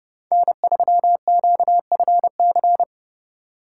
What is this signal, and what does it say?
Morse code, heard as a keyed tone, about 33 wpm, 700 Hz. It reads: N3QFC